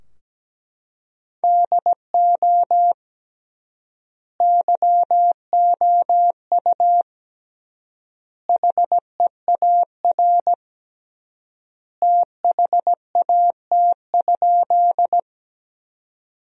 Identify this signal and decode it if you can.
Morse code, heard as a keyed tone, about 17 wpm, 700 Hz. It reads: DO YOU HEAR THAT?
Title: 3x Word-Spacing - Sentences from Top 300 Words - 17wpm